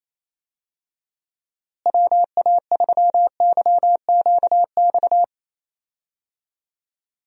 Morse code, heard as a keyed tone, about 28 wpm, 700 Hz. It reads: WA3YQX
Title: USA Call Signs - 6 Characters - Repeated 2x - 28wpm